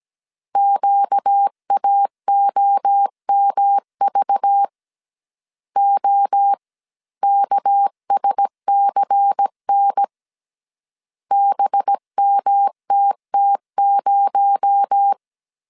Morse code, heard as a keyed tone, about 17 wpm, 800 Hz. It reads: QAOMV OKSCN BMTT0
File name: CW / DET / 1 6 1500C S A 16wpm 2635